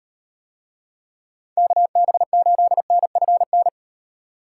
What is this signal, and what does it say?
Morse code, heard as a keyed tone, about 38 wpm, 700 Hz. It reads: KB8NFN